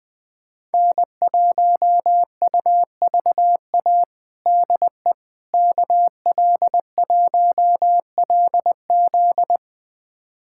Morse code, heard as a keyed tone, about 20 wpm, 700 Hz. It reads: N1UVA DE KL1LZ
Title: USA Callsign DE Callsign - 5 Characters - 20wpm